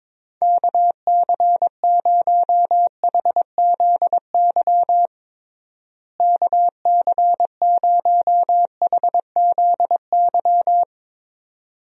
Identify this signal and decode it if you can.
Morse code, heard as a keyed tone, about 22 wpm, 700 Hz. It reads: KC0HZY KC0HZY
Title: USA Call Signs - 6 Characters - Repeated 2x - 22wpm